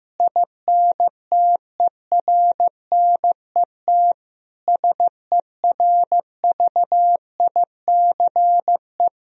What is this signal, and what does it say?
Morse code, heard as a keyed tone, about 15 wpm, 700 Hz. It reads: INTERNET SERVICE
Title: Sentences from Top 1000 Words - Review Part 4 - 15wpm